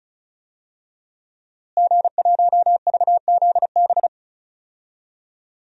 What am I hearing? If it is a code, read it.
Morse code, heard as a keyed tone, about 36 wpm, 700 Hz. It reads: G1VZB
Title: DXCC-100 Call Signs - 5 Characters - Repeated 3x - 35wpm